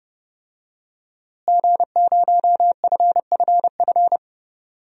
Morse code, heard as a keyed tone, about 30 wpm, 700 Hz. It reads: G0FFF
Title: DXCC-100 Call Signs - 5 Characters - Repeated 3x - 30wpm